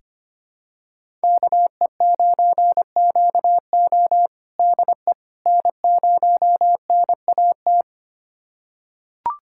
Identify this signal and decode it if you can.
Morse code, heard as a keyed tone, about 25 wpm, 700 Hz. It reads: KE9QO DE N0NAT
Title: USA Callsign DE Callsign - 5 Characters - 25wpm